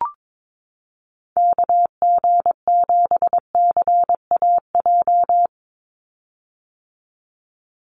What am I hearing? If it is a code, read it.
Morse code, heard as a keyed tone, about 22 wpm, 700 Hz. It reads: KG7CAJ
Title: USA Call Signs 22wpm